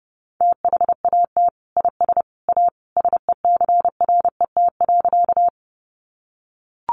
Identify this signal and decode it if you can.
Morse code, heard as a keyed tone, about 30 wpm, 700 Hz. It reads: THAT IS A SECRET.
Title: Sentences from Top 1200 Words 30wpm